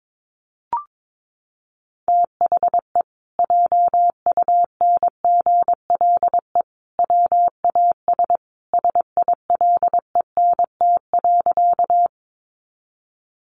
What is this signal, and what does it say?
Morse code, heard as a keyed tone, about 22 wpm, 700 Hz. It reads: THE JUNGLE WAS SILENT.
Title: A Sound Of Thunder 22wpm